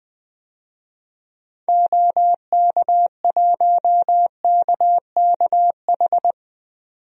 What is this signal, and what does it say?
Morse code, heard as a keyed tone, about 20 wpm, 700 Hz. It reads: OK1KKH
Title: DXCC-100 Call Signs - 6 Characters - 20wpm